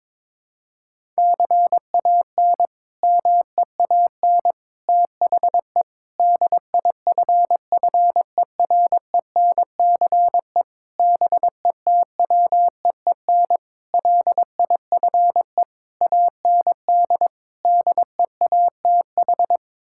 Morse code, heard as a keyed tone, about 22 wpm, 700 Hz. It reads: CAN MEAN THE DIFFERENCE BETWEEN LIFE AND DEATH
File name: practice / 22wpm / Sets of 8 Words - 22wpm